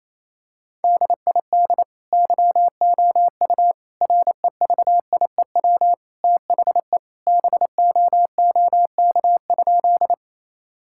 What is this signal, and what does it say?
Morse code, heard as a keyed tone, about 27 wpm, 700 Hz. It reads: DID YOU REVIEW THE BOOK?
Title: Sentences from Top 1000 Words - Review Part 6 - 28wpm